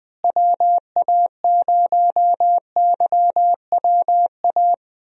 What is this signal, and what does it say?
Morse code, heard as a keyed tone, about 20 wpm, 700 Hz. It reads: WA0YWA
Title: USA Call Signs - 6 Characters - Repeated 2x - 20wpm